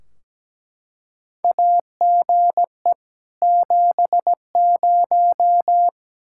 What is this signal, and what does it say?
Morse code, heard as a keyed tone, about 17 wpm, 700 Hz. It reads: AGE 70
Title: Practice QSO - Exchange 3 - More Info - 17wpm